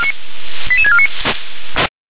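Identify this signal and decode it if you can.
Five-tone selective call (ZVEI), heard as quick runs of five tones, tones of about 70 ms. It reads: 88649
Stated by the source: ZVEI five-tone selective call 88088 received by SDR.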